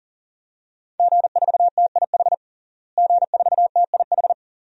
Morse code, heard as a keyed tone, about 40 wpm, 700 Hz. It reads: G4TIH G4TIH
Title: DXCC-100 Call Signs - 5 Characters - Repeated 2x - 40wpm